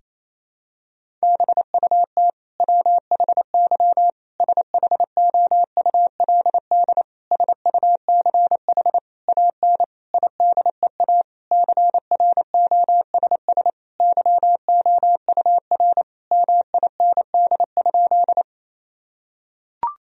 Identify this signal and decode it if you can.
Morse code, heard as a keyed tone, about 28 wpm, 700 Hz. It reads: BUT WHY SHOULD SUCH AN IDEA CROSS YOUR MIND?
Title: Sentences from Top 1000 Words - Review Part 1 - 28wpm